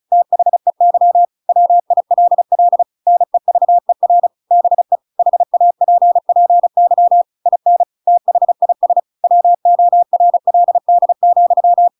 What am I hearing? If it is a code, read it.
Morse code, heard as a keyed tone, about 34 wpm, 700 Hz. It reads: THEY WILL NEVER BE HAPPY IN THIS WORLD,